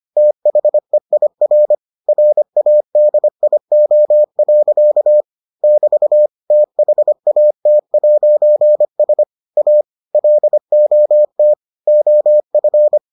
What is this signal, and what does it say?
Morse code, heard as a keyed tone, about 25 wpm, 600 Hz. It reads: THEIR RADIO. <BT> THAT'S A LOT OF